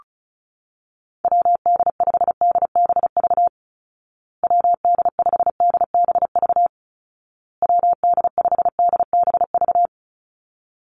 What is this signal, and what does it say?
Morse code, heard as a keyed tone, about 34 wpm, 700 Hz. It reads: WD5DBV WD5DBV WD5DBV
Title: Call Signs Repeated 2x - 35wpm